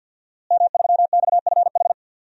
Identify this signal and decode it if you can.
Morse code, heard as a keyed tone, about 52 wpm, 700 Hz. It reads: M3XFH
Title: DXCC-100 Call Signs - 5 Characters - Repeated 2x - 50wpm